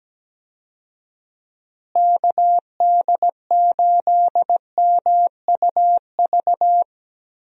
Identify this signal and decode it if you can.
Morse code, heard as a keyed tone, about 17 wpm, 700 Hz. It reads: KD8MUV
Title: USA Call Signs - 6 Characters - Repeated 3x - 17wpm